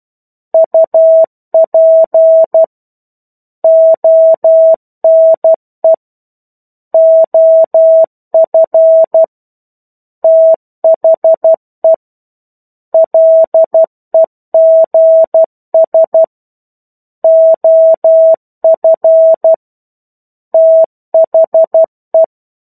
Morse code, heard as a keyed tone, about 12 wpm, 650 Hz. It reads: UP ONE OF THE LEGS OF THE
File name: 01-Chapter-01 12 WPM